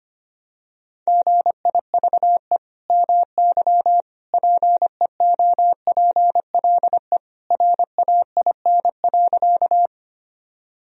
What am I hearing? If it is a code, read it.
Morse code, heard as a keyed tone, about 25 wpm, 700 Hz. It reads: GIVE MY PEOPLE RAIN.
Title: Sentences from Top 1000 Words - Review Part 1 - 25wpm